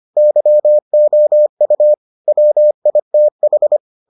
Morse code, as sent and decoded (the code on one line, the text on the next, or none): -.-- --- ..- / .-- .. - ....
YOU WITH